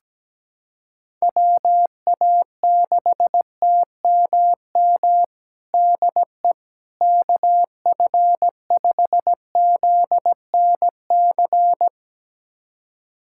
Morse code, heard as a keyed tone, about 17 wpm, 700 Hz. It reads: WA6TMM DE KF5ZNC
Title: USA Callsign DE Callsign - 6 Characters - 17wpm